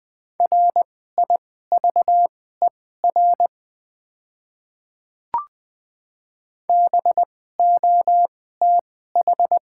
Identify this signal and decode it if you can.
Morse code, heard as a keyed tone, about 20 wpm, spaced as for 14 wpm, 700 Hz. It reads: RIVER BOTH
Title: 2x Character Spacing - Top 300 Words - 20wpm